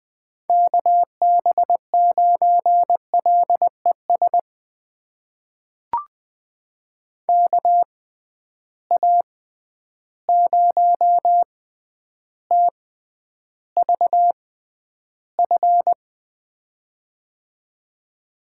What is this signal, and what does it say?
Morse code, heard as a keyed tone, about 20 wpm, spaced as for 10 wpm, 700 Hz. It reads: KB9LES KA0TVF
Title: USA Call Signs - 6 Characters - 6x Character Spacing - 20wpm